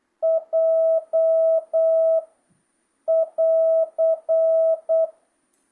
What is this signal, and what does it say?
Morse code, heard as a keyed tone, about 8 wpm, 650 Hz. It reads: J <AR>
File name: jh8cbh / 2025cw